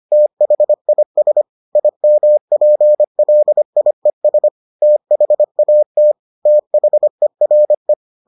Morse code, heard as a keyed tone, about 25 wpm, 600 Hz. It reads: THIS IMPLIES THAT THERE